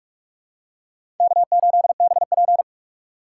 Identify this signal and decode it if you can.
Morse code, heard as a keyed tone, about 45 wpm, 700 Hz. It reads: K8BP
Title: USA Call Signs - 4 Characters - 4x Character Spacing - 45wpm